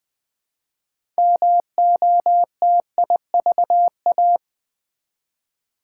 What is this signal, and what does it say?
Morse code, heard as a keyed tone, about 20 wpm, 700 Hz. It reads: MOTIVA
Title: Build A Word - 3 Letters At A Time - 8-Letter Words - 20wpm